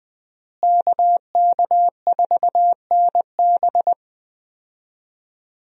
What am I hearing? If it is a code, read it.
Morse code, heard as a keyed tone, about 20 wpm, 700 Hz. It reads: KK4NB
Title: USA Call Signs - 5 Characters - Repeated 3x - 20wpm